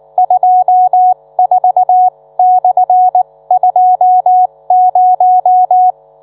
Morse code, heard as a keyed tone, about 19 wpm, spaced as for 16 wpm, 730 Hz. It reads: 24/20